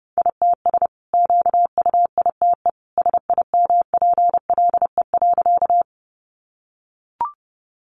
Morse code, heard as a keyed tone, about 30 wpm, 700 Hz. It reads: ITS QUITE SIMPLE.
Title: Sentences from Top 700 Words 30wpm